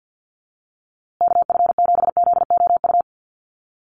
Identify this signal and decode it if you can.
Morse code, heard as a keyed tone, about 50 wpm, 700 Hz. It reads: KF7ZOU
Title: USA Call Signs Repeated 3x 50wpm